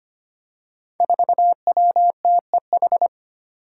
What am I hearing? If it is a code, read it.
Morse code, heard as a keyed tone, about 25 wpm, 700 Hz. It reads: 4WTEH
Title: Five Challenging Letter-Number Combinations 25wpm